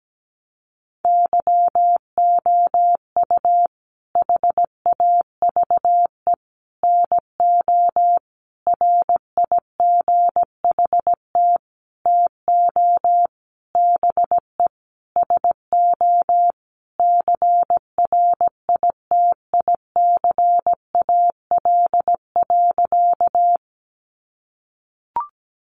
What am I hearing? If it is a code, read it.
Morse code, heard as a keyed tone, about 17 wpm, 700 Hz. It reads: YOU HAVE NO RIGHT TO BE SO CRITICAL.
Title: Sentences from Top 1100 Words 17wpm